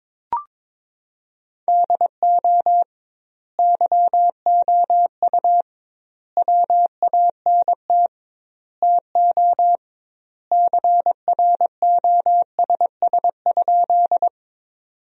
Morse code, heard as a keyed tone, about 22 wpm, 700 Hz. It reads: DO YOU WANT TO CROSS?
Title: 2x Word-Spacing - Sentences from Top 300 Words - 22wpm